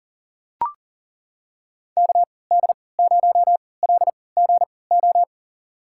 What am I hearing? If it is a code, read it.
Morse code, heard as a keyed tone, about 40 wpm, spaced as for 23 wpm, 700 Hz. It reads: KD0LGO
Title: USA Call Signs - 6 Characters - 3x Character Spacing - 40wpm